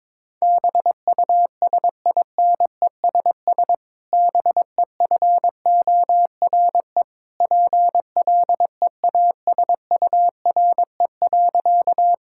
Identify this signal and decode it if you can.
Morse code, heard as a keyed tone, about 22 wpm, 700 Hz. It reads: BUSINESS BEFORE PLEASURE.
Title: Sentences from Top 1600 Words 22wpm